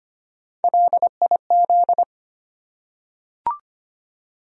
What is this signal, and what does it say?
Morse code, heard as a keyed tone, about 25 wpm, 700 Hz. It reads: LIZ
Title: USA Common Names - Max 3 Letters - 25wpm